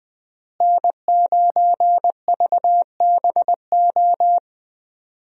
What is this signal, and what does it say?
Morse code, heard as a keyed tone, about 20 wpm, 700 Hz. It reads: N9VBO
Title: USA Call Signs - 5 Characters - Repeated 3x - 20wpm